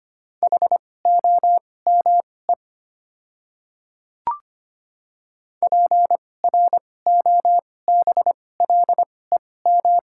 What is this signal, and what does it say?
Morse code, heard as a keyed tone, about 25 wpm, spaced as for 18 wpm, 700 Hz. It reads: HOME PROBLEM
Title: 2x Character Spacing - Top 500 Words - 25wpm